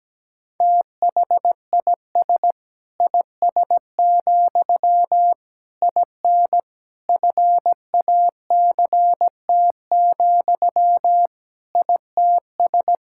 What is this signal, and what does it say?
Morse code, heard as a keyed tone, about 17 wpm, 700 Hz. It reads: THIS IS, IN FACT, ITS
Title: Sentences from Top 1000 Words - Review Part 5 - 17wpm